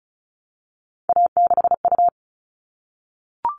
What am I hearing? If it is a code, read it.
Morse code, heard as a keyed tone, about 34 wpm, 700 Hz. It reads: A6U
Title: All Cut Numbers - 3-Digit 35wpm